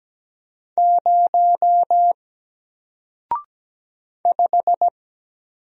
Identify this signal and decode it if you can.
Morse code, heard as a keyed tone, about 17 wpm, 700 Hz. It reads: 0 5
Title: Single Letter-Number Alt 17wpm